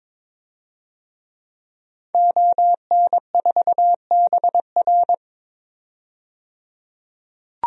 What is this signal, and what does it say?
Morse code, heard as a keyed tone, about 21 wpm, 700 Hz. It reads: ON4BR
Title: DXCC-100 Call Signs - 5 Characters - 22wpm